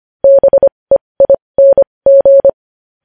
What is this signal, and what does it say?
Morse code, heard as a keyed tone, about 25 wpm, spaced as for 20 wpm, 550 Hz. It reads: BEING